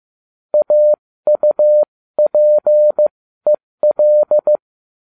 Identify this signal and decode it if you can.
Morse code, heard as a keyed tone, about 15 wpm, spaced as for 13 wpm, 600 Hz. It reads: AUPEL